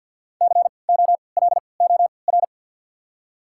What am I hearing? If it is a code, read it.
Morse code, heard as a keyed tone, about 53 wpm, spaced as for 29 wpm, 700 Hz. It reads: COLOR
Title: 3x Character Spacing - Top 500 Words - 50wpm